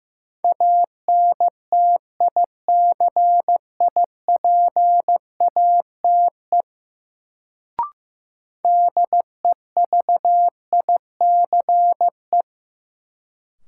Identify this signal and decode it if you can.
Morse code, heard as a keyed tone, about 15 wpm, 700 Hz. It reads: ANTICIPATE DEVICE
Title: Sentences from Top 1500 Words - New Word Drill Alt - 15wpm